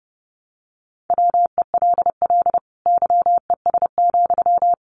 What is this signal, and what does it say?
Morse code, heard as a keyed tone, about 30 wpm, 700 Hz. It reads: WELL YES,
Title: Sentences from Top 500 Words 30wpm